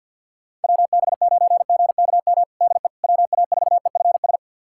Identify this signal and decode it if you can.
Morse code, heard as a keyed tone, about 47 wpm, 700 Hz. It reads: WB9GKM DE WA4EFS
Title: USA Callsign DE Callsign - 6 Characters - 50wpm